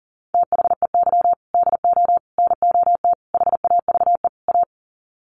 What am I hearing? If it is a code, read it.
Morse code, heard as a keyed tone, about 40 wpm, 700 Hz. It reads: THEY DO NOT HAVE A
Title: Sets of 5 Words 40wpm